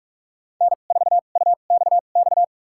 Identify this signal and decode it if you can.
Morse code, heard as a keyed tone, about 42 wpm, spaced as for 31 wpm, 690 Hz. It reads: N4UXX
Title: USA Call Signs - 5 Characters - 2x Character Spacing - 45wpm